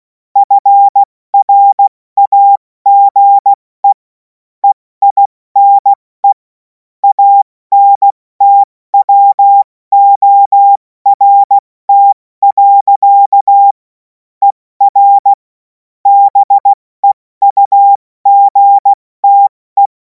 Morse code, heard as a keyed tone, about 16 wpm, 800 Hz. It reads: FRAGE EINE ANTWORT. ER BEUGTE
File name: Morsekurs / Witze / Witz 24. März 2026 70 F80